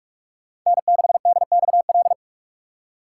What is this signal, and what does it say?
Morse code, heard as a keyed tone, about 43 wpm, 700 Hz. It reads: N6DXL
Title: USA Call Signs - 5 Characters - 3x Character Spacing - 45wpm